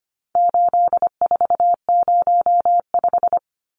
Morse code, heard as a keyed tone, about 25 wpm, 700 Hz. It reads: 8405
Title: Numbers - 4-Digit 25wpm